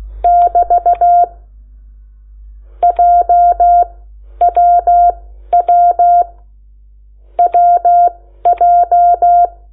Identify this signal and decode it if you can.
Morse code, heard as a keyed tone, about 16 wpm, spaced as for 9 wpm, 700 Hz. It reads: <BT> JWWWJ
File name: Lecciones / CLASE Nº8